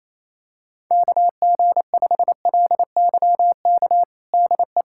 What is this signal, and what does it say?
Morse code, heard as a keyed tone, about 28 wpm, 700 Hz. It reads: KG5LYK DE
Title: USA Callsign DE Callsign - 6 Characters - 28wpm